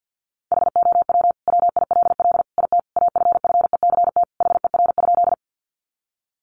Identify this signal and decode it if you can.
Morse code, heard as a keyed tone, about 50 wpm, 700 Hz. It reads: HOW WILL IT AFFECT HER?
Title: Sentences from Top 900 Words 50wpm